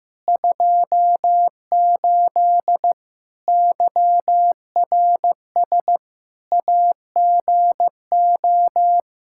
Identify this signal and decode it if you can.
Morse code, heard as a keyed tone, about 15 wpm, 700 Hz. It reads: 28 YRS AGO
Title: Practice QSO - Exchange 3 - More Info - 15wpm